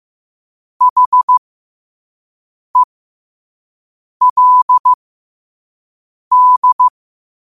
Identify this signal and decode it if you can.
Morse code, heard as a keyed tone, about 15 wpm, spaced as for 5 wpm, 1000 Hz. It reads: HELD
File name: qotdfiles / 5wpm